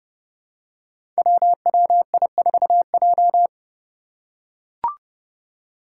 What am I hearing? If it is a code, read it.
Morse code, heard as a keyed tone, about 30 wpm, 700 Hz. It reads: WWI4J
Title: Five Challenging Letter-Number Combinations 30wpm